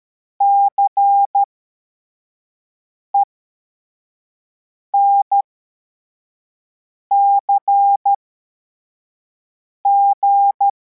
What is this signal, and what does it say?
Morse code, heard as a keyed tone, about 13 wpm, spaced as for 4 wpm, 800 Hz. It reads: CENCG